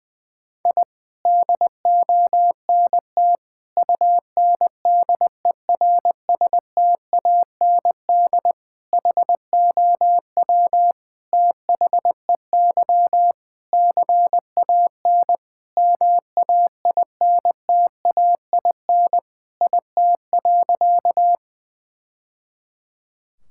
Morse code, heard as a keyed tone, about 21 wpm, 700 Hz. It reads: I DONT UNDERSTAND HOW THEY CAN MAINTAIN IT.
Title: Sentences from Top 1000 Words - Review Part 5 - 20wpm